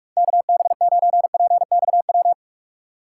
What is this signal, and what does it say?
Morse code, heard as a keyed tone, about 46 wpm, 700 Hz. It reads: KB9PXW